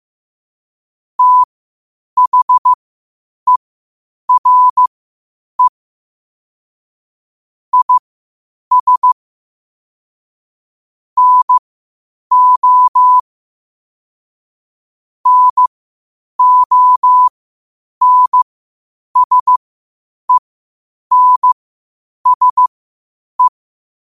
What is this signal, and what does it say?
Morse code, heard as a keyed tone, about 15 wpm, spaced as for 8 wpm, 1000 Hz. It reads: THERE IS NO NONSENSE
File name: qotdfiles / 8wpm